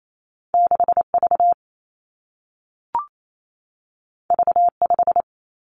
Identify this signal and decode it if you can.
Morse code, heard as a keyed tone, about 28 wpm, 700 Hz. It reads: 6 V V 5
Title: Two Challenging Letter-Number Combinations 28wpm